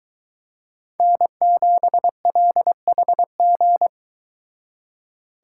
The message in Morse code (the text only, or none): N7LHG